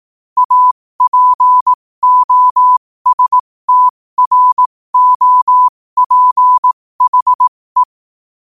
Morse code, heard as a keyed tone, about 18 wpm, spaced as for 15 wpm, 1000 Hz. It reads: APOSTROPHE
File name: qotdfiles / 15wpm